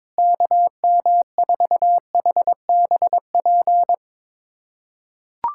Morse code, heard as a keyed tone, about 22 wpm, 700 Hz. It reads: KM4HBP
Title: USA Callsign DE Callsign - 6 Characters - 22wpm